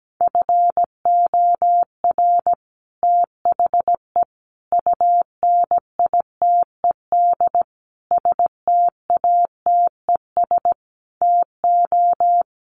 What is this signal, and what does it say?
Morse code, heard as a keyed tone, about 17 wpm, 700 Hz. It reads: FOR THE UNITED STATES TO
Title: Sets of 5 Words Encore 17wpm